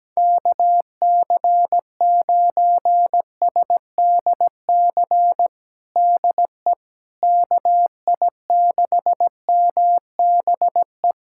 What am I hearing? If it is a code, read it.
Morse code, heard as a keyed tone, about 17 wpm, 700 Hz. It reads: KC9SDC DE KI6MBE
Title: USA Callsign DE Callsign - 6 Characters - 17wpm